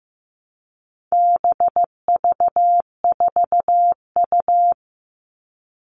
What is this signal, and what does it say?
Morse code, heard as a keyed tone, about 15 wpm, 700 Hz. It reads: BV4U
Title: All Cut Numbers - 4-Digit 15wpm